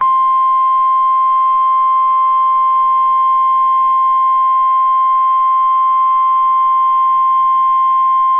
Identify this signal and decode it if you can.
Emergency Alert System (SAME) tones (the 1050 Hz attention tone). It is heard from 0.7 to 8.0 s.